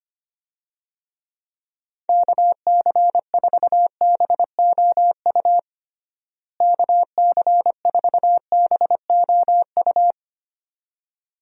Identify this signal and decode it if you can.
Morse code, heard as a keyed tone, about 25 wpm, 700 Hz. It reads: KC4BOU KC4BOU
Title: USA Call Signs - 6 Characters - Repeated 2x - 25wpm